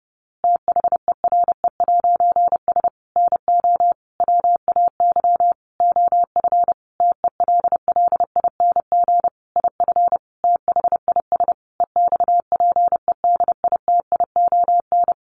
Morse code, heard as a keyed tone, about 30 wpm, 700 Hz. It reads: THERE'S NO WAY OF TELLING IF THIS EXPEDITION
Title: A Sound Of Thunder 30wpm